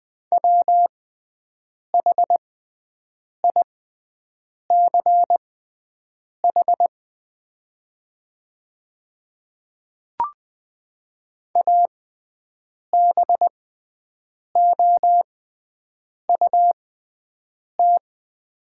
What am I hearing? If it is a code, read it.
Morse code, heard as a keyed tone, about 20 wpm, spaced as for 7 wpm, 700 Hz. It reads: WHICH ABOUT
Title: 6x Character Spacing - Top 300 Words - 20wpm